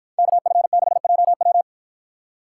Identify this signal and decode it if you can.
Morse code, heard as a keyed tone, about 53 wpm, 700 Hz. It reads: KF6JW